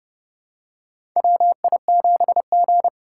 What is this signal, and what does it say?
Morse code, heard as a keyed tone, about 32 wpm, 700 Hz. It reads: WI7G
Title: USA Call Signs - 4 Characters - 2x Character Spacing - 30wpm